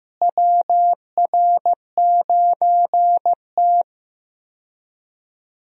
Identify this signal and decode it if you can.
Morse code, heard as a keyed tone, about 15 wpm, 700 Hz. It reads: WR9T
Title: USA Call Signs - 4 Characters - Repeated 2x - 15wpm